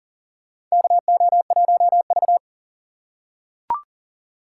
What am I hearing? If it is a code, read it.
Morse code, heard as a keyed tone, about 40 wpm, 700 Hz. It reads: KO1V
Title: USA Call Signs - 4 Characters - 8x Character Spacing - 40wpm